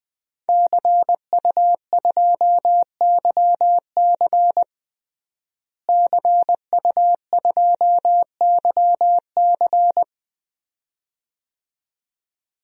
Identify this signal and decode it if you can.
Morse code, heard as a keyed tone, about 20 wpm, 700 Hz. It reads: CU2YC CU2YC
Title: DXCC-100 Call Signs - 5 Characters - Repeated 2x - 20wpm